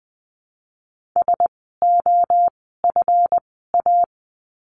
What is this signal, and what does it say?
Morse code, heard as a keyed tone, about 20 wpm, spaced as for 14 wpm, 700 Hz. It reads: SOFA
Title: 139 Lesson 23f - Letter f - Words 20wpm